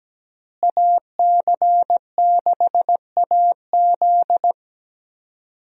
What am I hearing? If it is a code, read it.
Morse code, heard as a keyed tone, about 17 wpm, 700 Hz. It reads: AC6AZ
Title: USA Call Signs - 5 Characters - Repeated 3x - 17wpm